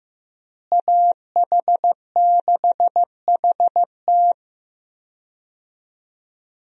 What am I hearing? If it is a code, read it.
Morse code, heard as a keyed tone, about 15 wpm, 700 Hz. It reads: AH6HT
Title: USA Call Signs - 5 Characters - Repeated 3x - 15wpm